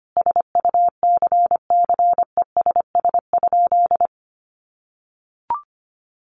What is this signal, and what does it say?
Morse code, heard as a keyed tone, about 25 wpm, 700 Hz. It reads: SUCCESS?
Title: Sentences from Top 900 Words 25wpm